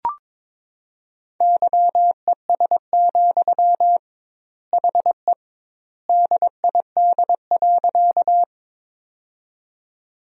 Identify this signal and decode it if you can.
Morse code, heard as a keyed tone, about 22 wpm, 700 Hz. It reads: YES, HE DID.
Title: 2x Word-Spacing - Sentences from Top 500 Words - 22wpm